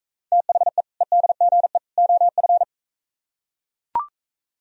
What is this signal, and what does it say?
Morse code, heard as a keyed tone, about 42 wpm, 700 Hz. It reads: THE EDGE OF